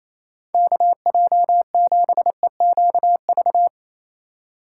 Morse code, heard as a keyed tone, about 31 wpm, 700 Hz. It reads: KJ7EQV
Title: Speed Racing - USA 2x3 Call Signs 28wpm